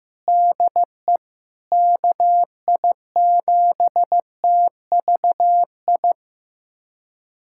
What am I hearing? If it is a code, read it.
Morse code, heard as a keyed tone, about 15 wpm, 700 Hz. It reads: DE KI7TVI
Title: USA Callsign DE Callsign - 6 Characters - 15wpm